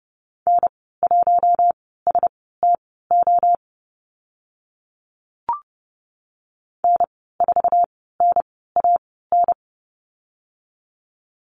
Morse code, heard as a keyed tone, about 30 wpm, spaced as for 17 wpm, 700 Hz. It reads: N1STO N4NAN
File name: Lesson 14g - Number 2 - Call Signs 30wpm